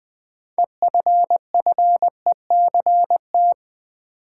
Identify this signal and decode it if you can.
Morse code, heard as a keyed tone, about 21 wpm, 700 Hz. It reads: EFFECT
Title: Sentences from Top 1000 Words - New Word Drill - 20wpm